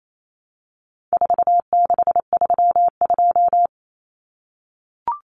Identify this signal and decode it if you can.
Morse code, heard as a keyed tone, about 28 wpm, 700 Hz. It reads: 4632
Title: Common Cut Numbers - 4-Digit 28wpm